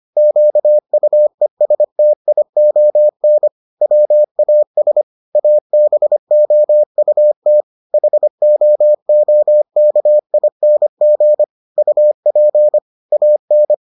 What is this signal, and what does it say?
Morse code, heard as a keyed tone, about 25 wpm, 600 Hz. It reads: QUESTION WAS ABOUT HOOKING UP AN